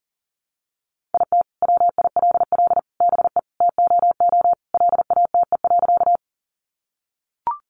Morse code, heard as a keyed tone, about 40 wpm, 700 Hz. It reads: IT WILL BE TOO LATE.
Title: Sentences from Top 300 Words 40wpm